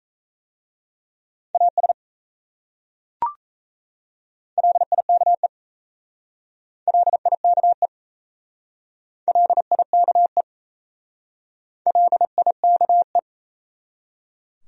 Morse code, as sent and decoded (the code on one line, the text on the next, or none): .- ... / .-.. .. -.- . / .-.. .. -.- . / .-.. .. -.- . / .-.. .. -.- .
AS LIKE LIKE LIKE LIKE